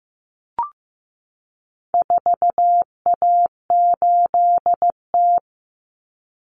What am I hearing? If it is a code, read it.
Morse code, heard as a keyed tone, about 15 wpm, 700 Hz. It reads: 4A8T
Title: Common Cut Numbers - 4-Digit 15wpm